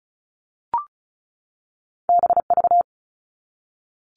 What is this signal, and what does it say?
Morse code, heard as a keyed tone, about 36 wpm, 700 Hz. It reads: BV
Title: All Cut Numbers - 2-Digit 35wpm